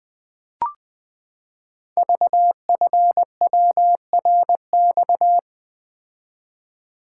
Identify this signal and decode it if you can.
Morse code, heard as a keyed tone, about 20 wpm, 700 Hz. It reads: VFWRX
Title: Code Groups - 5 Letters - 20wpm